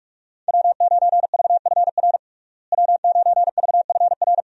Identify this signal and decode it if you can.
Morse code, heard as a keyed tone, about 46 wpm, 700 Hz. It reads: W9VFR W9VFR
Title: USA Call Signs - 5 Characters - Repeated 3x - 45wpm